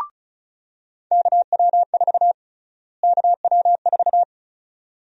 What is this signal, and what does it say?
Morse code, heard as a keyed tone, about 36 wpm, 700 Hz. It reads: KW4 KW4
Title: USA Call Sign Prefixes - Repeated 2x - 35wpm